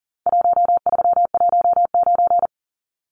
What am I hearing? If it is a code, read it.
Morse code, heard as a keyed tone, about 40 wpm, 700 Hz. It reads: 1319
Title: Numbers - Contest Style - 4-Digit 40wpm